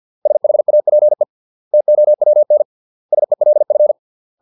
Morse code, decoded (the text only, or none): SHAPE TOWN SELF